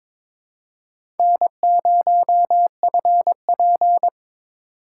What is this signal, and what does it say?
Morse code, heard as a keyed tone, about 22 wpm, 700 Hz. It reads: N0FP
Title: USA Call Signs - 4 Characters - 2x Character Spacing - 22wpm